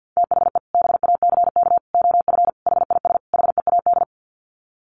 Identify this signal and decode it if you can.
Morse code, heard as a keyed tone, about 50 wpm, 700 Hz. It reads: THE BACK OF HIS HEAD
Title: Sets of 5 Words Encore 50wpm